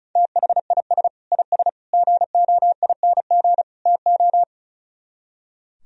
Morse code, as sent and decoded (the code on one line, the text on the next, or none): - .... .. ... / .. ... / --. --- .. -. --. / - ---
THIS IS GOING TO